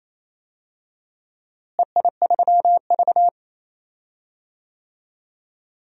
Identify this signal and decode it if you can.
Morse code, heard as a keyed tone, about 28 wpm, 700 Hz. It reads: EI3V
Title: DXCC-100 Call Signs - 4 Characters - Repeated 3x - 28wpm